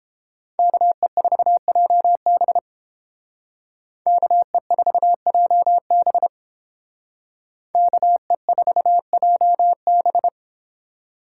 Morse code, heard as a keyed tone, about 31 wpm, 700 Hz. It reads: KE4JB KE4JB KE4JB